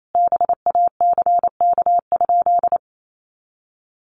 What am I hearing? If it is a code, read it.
Morse code, heard as a keyed tone, about 28 wpm, 700 Hz. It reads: BACK?
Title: Sentences from Top 400 Words 28wpm